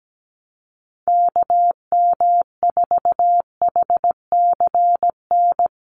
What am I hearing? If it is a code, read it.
Morse code, heard as a keyed tone, about 17 wpm, 700 Hz. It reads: KM4HCN
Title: USA Call Signs Repeated 2x 17wpm